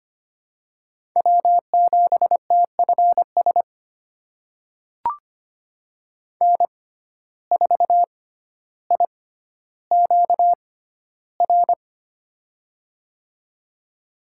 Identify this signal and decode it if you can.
Morse code, heard as a keyed tone, about 25 wpm, spaced as for 13 wpm, 700 Hz. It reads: W7TFS N4IQR
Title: USA Call Signs - 6x Character Spacing - 25wpm